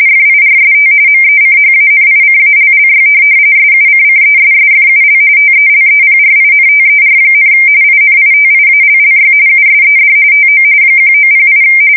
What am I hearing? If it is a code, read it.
RTTY, about 45.45 baud. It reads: FIRST MECHANICALLY SUCCESSFUL TELEPRINTER, INITIALLY USED TO 1908
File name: fsk170 / teletype-corp